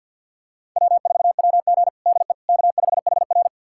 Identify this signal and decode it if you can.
Morse code, heard as a keyed tone, about 52 wpm, 700 Hz. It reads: W4WG DE K5HR
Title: USA Callsign DE Callsign - 4 Characters - 50wpm